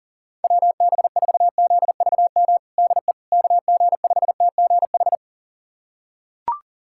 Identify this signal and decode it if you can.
Morse code, heard as a keyed tone, about 40 wpm, 700 Hz. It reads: WB4ZVM DE KG5TGH
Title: USA Callsign DE Callsign - 6 Characters - 40wpm